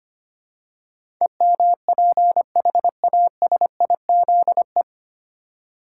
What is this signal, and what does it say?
Morse code, heard as a keyed tone, about 25 wpm, 700 Hz. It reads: EMPHASIZE
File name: practice / 25wpm / 9 Letter Words - First 3 Letters Given - 25wpm